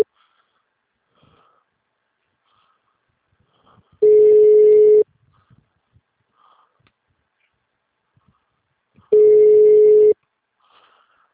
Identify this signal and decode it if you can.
Ringback tone (Europe, 425 Hz), with speech.
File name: out- -2007- - - .1245